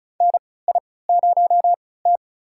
Morse code, heard as a keyed tone, about 35 wpm, spaced as for 20 wpm, 700 Hz. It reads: NI0T
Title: USA Call Signs - 4 Characters - 3x Character Spacing - 35wpm